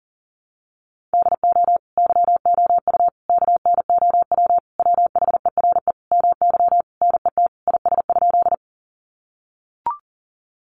Morse code, heard as a keyed tone, about 40 wpm, 700 Hz. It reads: DO YOU KNOW WHERE MY NET IS?
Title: Sentences from Top 1400 Words 40wpm